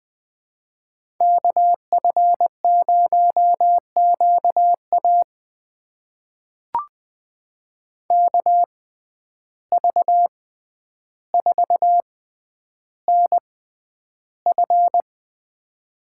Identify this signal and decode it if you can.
Morse code, heard as a keyed tone, about 20 wpm, spaced as for 10 wpm, 700 Hz. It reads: KF0QA KV4NF
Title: USA Call Signs - 5 Characters - 6x Character Spacing - 20wpm